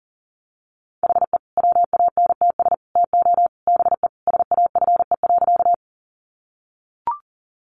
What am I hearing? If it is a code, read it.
Morse code, heard as a keyed tone, about 40 wpm, 700 Hz. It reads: HE WANTS TO BE SAFE.
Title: Additional Easy Phrases 40wpm